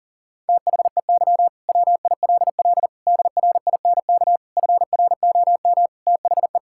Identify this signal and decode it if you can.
Morse code, heard as a keyed tone, about 40 wpm, 700 Hz. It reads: THEY WILL DRINK FROM THE
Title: Sentences from Top 1600 Words 40wpm